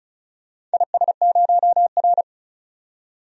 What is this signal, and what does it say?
Morse code, heard as a keyed tone, about 36 wpm, 700 Hz. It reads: IS0R
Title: DXCC-100 Call Signs - 4 Characters - 35wpm